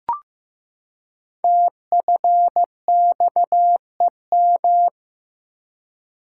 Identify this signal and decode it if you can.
Morse code, heard as a keyed tone, about 15 wpm, 700 Hz. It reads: TFXEM
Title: Code Groups - 5 Letters - 15wpm